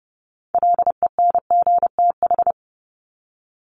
Morse code, heard as a keyed tone, about 30 wpm, 700 Hz. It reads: LENGTH
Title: Top 751-1000 Words 30wpm